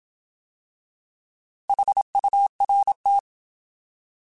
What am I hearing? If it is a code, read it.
Morse code, heard as a keyed tone, about 27 wpm, 800 Hz. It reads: HURT
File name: Alphabetical / Alphabetical mp3s / Hs @25